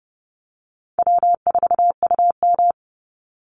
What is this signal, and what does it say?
Morse code, heard as a keyed tone, about 30 wpm, 700 Hz. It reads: W4UM
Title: Easy Call Signs - 30wpm